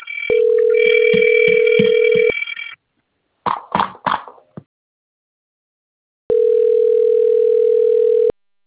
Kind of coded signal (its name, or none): telephone call progress tone